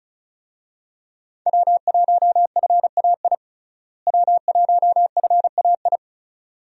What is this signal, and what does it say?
Morse code, heard as a keyed tone, about 36 wpm, 700 Hz. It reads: W1FAI W1FAI
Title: USA Call Signs - 5 Characters - Repeated 2x - 35wpm